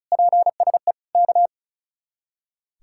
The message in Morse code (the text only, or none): PSE K